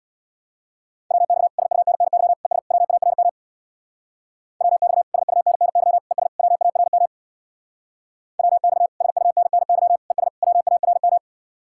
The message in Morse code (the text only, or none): WX SUNNY ES WARM WX SUNNY ES WARM WX SUNNY ES WARM